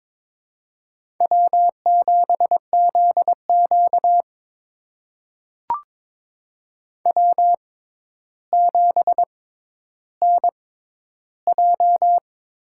Morse code, heard as a keyed tone, about 22 wpm, spaced as for 11 wpm, 700 Hz. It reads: W7ZQ W7NJ
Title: USA Call Signs - 6x Character Spacing - 22wpm